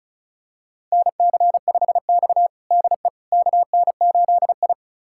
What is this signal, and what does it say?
Morse code, heard as a keyed tone, about 34 wpm, 700 Hz. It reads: NC5X DE KN8I
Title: USA Callsign DE Callsign - 4 Characters - 35wpm